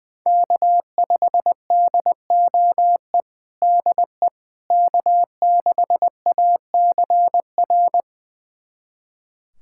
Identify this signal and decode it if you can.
Morse code, heard as a keyed tone, about 21 wpm, 700 Hz. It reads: K5DOE DE K6ACR
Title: USA Callsign DE Callsign - 5 Characters - 20wpm